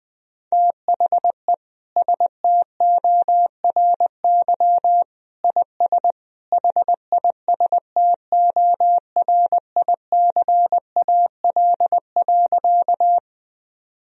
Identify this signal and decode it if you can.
Morse code, heard as a keyed tone, about 20 wpm, 700 Hz. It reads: THE STORY IS HISTORICAL.
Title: Sentences from Top 1000 Words - Review Part 4 - 20wpm